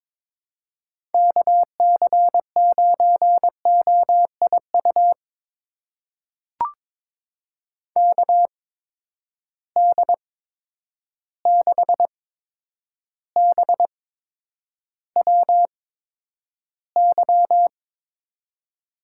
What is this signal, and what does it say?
Morse code, heard as a keyed tone, about 22 wpm, spaced as for 9 wpm, 700 Hz. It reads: KC9OIU KD6BWY
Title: USA Call Signs - 6 Characters - 8x Character Spacing - 22wpm